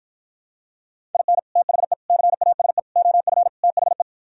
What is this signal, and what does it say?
Morse code, heard as a keyed tone, about 52 wpm, 700 Hz. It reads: IN THE CASE OF THE